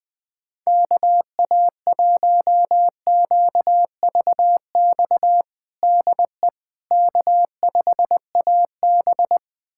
Morse code, heard as a keyed tone, about 21 wpm, 700 Hz. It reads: KA1QVX DE K5AB
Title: USA Callsign DE Callsign 20wpm